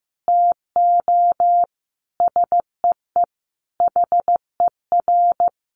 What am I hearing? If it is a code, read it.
Morse code, heard as a keyed tone, about 15 wpm, 700 Hz. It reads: TO SEE HER
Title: Sentences from Top 500 Words 15wpm